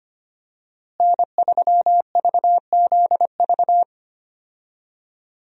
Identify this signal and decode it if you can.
Morse code, heard as a keyed tone, about 25 wpm, 700 Hz. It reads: N3VZV
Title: USA Call Signs - 5 Characters - Repeated 2x - 25wpm